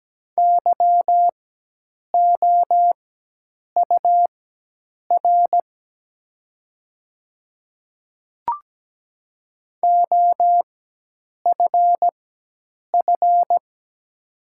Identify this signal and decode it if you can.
Morse code, heard as a keyed tone, about 17 wpm, spaced as for 8 wpm, 700 Hz. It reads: YOUR OFF
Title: 4x Character Spacing - Top 200 Words - 17wpm